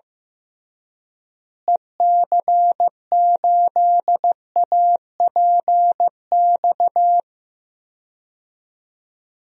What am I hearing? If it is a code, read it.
Morse code, heard as a keyed tone, about 15 wpm, 700 Hz. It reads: EC8APX
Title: DXCC-100 Call Signs - 6 Characters - Repeated 3x - 15wpm